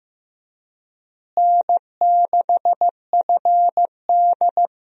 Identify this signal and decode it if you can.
Morse code, heard as a keyed tone, about 15 wpm, 700 Hz. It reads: N6FD
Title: USA Call Signs - 4 Characters - Repeated 2x - 15wpm